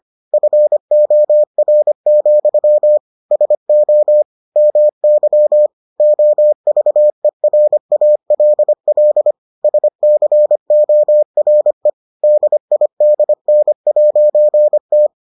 Morse code, heard as a keyed tone, about 25 wpm, 600 Hz. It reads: FOR, SO MY OVERALL SCORE DIDN'T